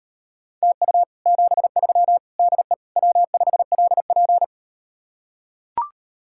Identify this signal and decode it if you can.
Morse code, heard as a keyed tone, about 38 wpm, 700 Hz. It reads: TU 73 DE W5LP